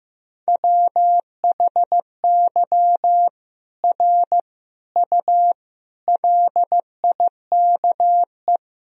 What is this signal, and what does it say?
Morse code, heard as a keyed tone, about 15 wpm, 700 Hz. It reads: WHY R U LIKE
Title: Sentences from Top 200 Words and Abbreviations 15wpm